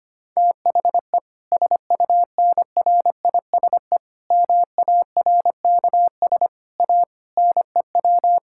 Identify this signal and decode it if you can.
Morse code, heard as a keyed tone, about 25 wpm, 700 Hz. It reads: THE SUNRISE MARKS A NEW
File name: practice / 25wpm / Difficult Sentences with Hints - 25wpm